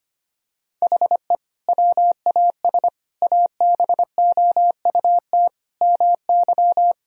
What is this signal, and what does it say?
Morse code, heard as a keyed tone, about 25 wpm, 700 Hz. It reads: HE WAS ABOUT MY
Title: Sentences from Top 1000 Words - Review Part 2 - 25wpm